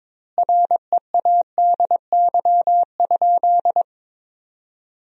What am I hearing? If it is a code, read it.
Morse code, heard as a keyed tone, about 22 wpm, 700 Hz. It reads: READY?
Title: 4x Word-Spacing - Sentences from Top 400 Words - 22wpm